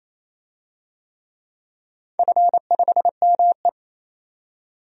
Morse code, heard as a keyed tone, about 28 wpm, 700 Hz. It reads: F5ME
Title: DXCC-100 Call Signs - 4 Characters - 28wpm